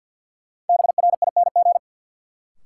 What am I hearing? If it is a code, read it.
Morse code, heard as a keyed tone, about 53 wpm, 700 Hz. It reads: BRING
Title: Sets of 5 Words Encore - New Word Drill - 50wpm